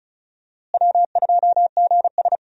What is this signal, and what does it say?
Morse code, heard as a keyed tone, about 36 wpm, 700 Hz. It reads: W2GS
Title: USA Call Signs - 4 Characters - Repeated 2x - 35wpm